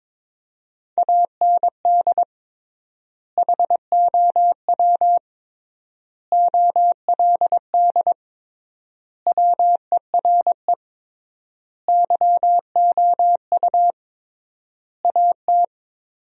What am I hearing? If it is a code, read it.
Morse code, heard as a keyed tone, about 21 wpm, 700 Hz. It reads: AND HOW OLD WERE YOU AT
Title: 3x Word-Spacing - Sentences from Top 200 Words - 22wpm